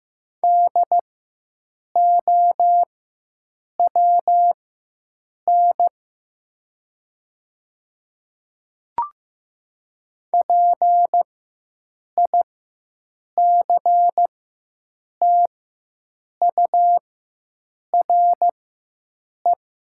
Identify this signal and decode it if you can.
Morse code, heard as a keyed tone, about 15 wpm, spaced as for 7 wpm, 700 Hz. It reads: DOWN PICTURE
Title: 4x Character Spacing - Top 300 Words - 15wpm